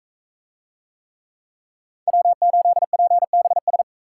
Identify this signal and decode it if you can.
Morse code, heard as a keyed tone, about 42 wpm, 700 Hz. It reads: W8PBS